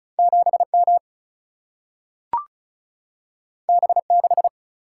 Morse code, heard as a keyed tone, about 36 wpm, 700 Hz. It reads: 7M B6
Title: Single-Letter Single-Number 35wpm